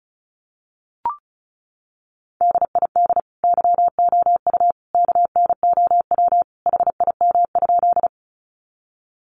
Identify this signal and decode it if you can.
Morse code, heard as a keyed tone, about 34 wpm, 700 Hz. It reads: DID YOU KNOW HIM?
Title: Sentences from Top 100 Words 35wpm